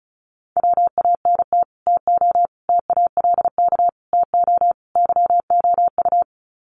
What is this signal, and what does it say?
Morse code, heard as a keyed tone, about 34 wpm, 700 Hz. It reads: WANT TO TALK TO YOU
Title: Sets of 5 Words Encore 35wpm